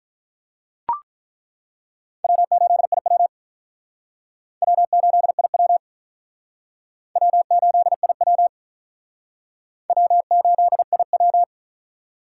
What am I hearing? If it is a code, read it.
Morse code, heard as a keyed tone, about 48 wpm, 700 Hz. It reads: W8IW W8IW W8IW W8IW